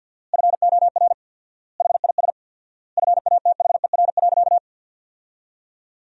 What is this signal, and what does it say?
Morse code, heard as a keyed tone, about 47 wpm, 700 Hz. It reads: FOR HIS FATHER.
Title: 4x Word-Spacing - Sentences from Top 500 Words - 50wpm